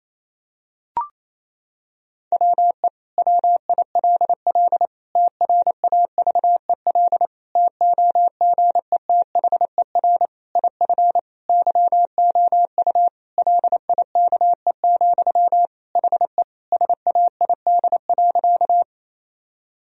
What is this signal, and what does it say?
Morse code, heard as a keyed tone, about 28 wpm, 700 Hz. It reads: WE WILL TRAVEL TOGETHER IF YOU LIKE, HE SAID.
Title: Sentences from Top 1000 Words - Review Part 2 - 28wpm